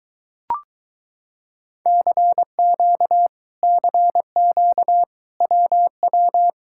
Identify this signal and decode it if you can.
Morse code, heard as a keyed tone, about 23 wpm, 700 Hz. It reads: CQ CQ WW